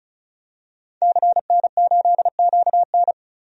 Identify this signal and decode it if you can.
Morse code, heard as a keyed tone, about 36 wpm, 700 Hz. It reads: CN8QN
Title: DXCC-100 Call Signs - 5 Characters - 35wpm